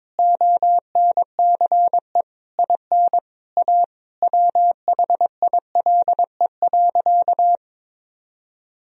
Morse code, heard as a keyed tone, about 22 wpm, 700 Hz. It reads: ONCE IN A WHILE.
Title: Sentences from Top 500 Words and Abbreviations 22wpm